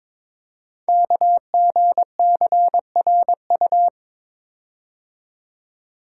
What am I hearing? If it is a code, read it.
Morse code, heard as a keyed tone, about 22 wpm, 700 Hz. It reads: KGCRU
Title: Code Groups - 5 Letters - 22wpm